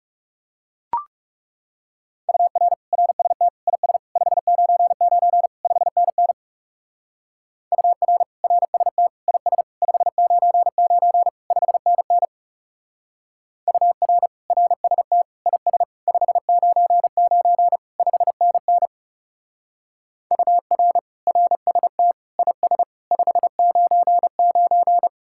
Morse code, heard as a keyed tone, about 37 wpm, 700 Hz. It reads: UR RST IS 599 5NN UR RST IS 599 5NN UR RST IS 599 5NN UR RST IS 599
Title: Speed-Racing Common QSO Phrases - 30wpm